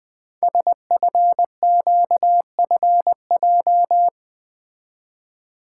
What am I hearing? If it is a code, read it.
Morse code, heard as a keyed tone, about 20 wpm, 700 Hz. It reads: SFQFJ
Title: Code Groups - 5 Letters - 20wpm